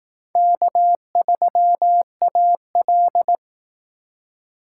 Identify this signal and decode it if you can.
Morse code, heard as a keyed tone, about 18 wpm, 700 Hz. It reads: K3AL
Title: Speed Racing - USA 1x2 Call Signs 15wpm